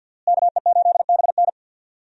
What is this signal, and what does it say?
Morse code, heard as a keyed tone, about 52 wpm, 700 Hz. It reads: KE8BN
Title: USA Call Signs - 5 Characters - Repeated 3x - 50wpm